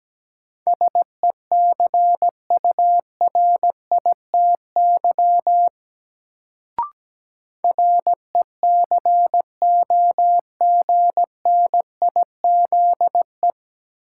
Morse code, heard as a keyed tone, about 17 wpm, 700 Hz. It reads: SECURITY RECOGNIZE
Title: Sentences from Top 1400 Words - New Word Drill Alt - 17wpm